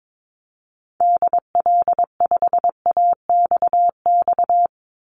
Morse code, heard as a keyed tone, about 22 wpm, 700 Hz. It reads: DL5AXX
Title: Call Signs - 22wpm